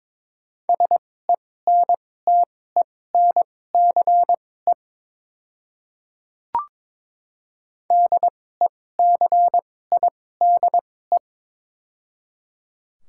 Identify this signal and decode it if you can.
Morse code, heard as a keyed tone, about 22 wpm, spaced as for 16 wpm, 700 Hz. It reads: SENTENCE DECIDE
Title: 2x Character Spacing - Top 500 Words - 22wpm